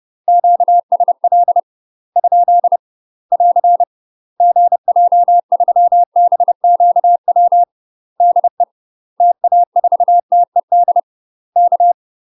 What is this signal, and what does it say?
Morse code, heard as a keyed tone, about 30 wpm, 700 Hz. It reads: QSL ? <AR> GJ3BQW DE TA4TED K
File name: UK MASTER QSOs TRAINING / 30WPM / UK MASTER QSOs 0043